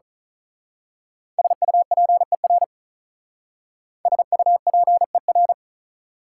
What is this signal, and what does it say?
Morse code, heard as a keyed tone, about 39 wpm, 700 Hz. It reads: SUPER SUPER
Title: Speed-Racing Prefixes and Suffixes 35wpm